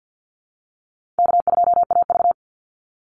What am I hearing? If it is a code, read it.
Morse code, heard as a keyed tone, about 50 wpm, 700 Hz. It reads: K2AV
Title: Easy Call Signs - 50wpm